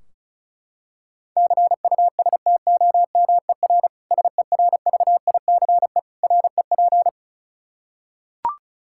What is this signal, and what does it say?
Morse code, heard as a keyed tone, about 34 wpm, 700 Hz. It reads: CUSTOMER SERVICE REP
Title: Practice QSO - Exchange 3 - More Info - 35wpm